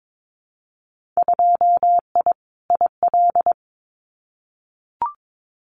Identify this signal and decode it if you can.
Morse code, heard as a keyed tone, about 22 wpm, 700 Hz. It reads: 2I IL
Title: Winter Field Day Exchanges 22wpm